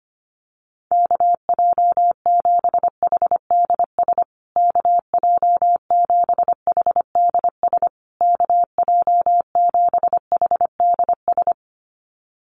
Morse code, heard as a keyed tone, about 25 wpm, 700 Hz. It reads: KJ7HDS KJ7HDS KJ7HDS
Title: USA Call Signs Repeated 3x 25wpm